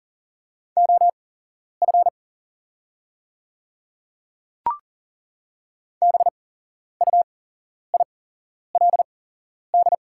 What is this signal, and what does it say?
Morse code, heard as a keyed tone, about 40 wpm, spaced as for 11 wpm, 690 Hz. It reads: OF BUILD
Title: 8x Character Spacing - Top 300 Words - 40wpm